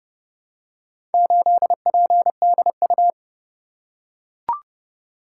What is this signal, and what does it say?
Morse code, heard as a keyed tone, about 27 wpm, 700 Hz. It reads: 8PDU